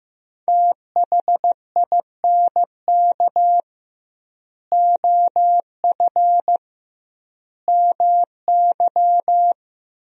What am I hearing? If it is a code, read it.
Morse code, heard as a keyed tone, about 16 wpm, 700 Hz. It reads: THINK OF MY
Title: 2x Word-Spacing - Sentences from Top 300 Words - 15wpm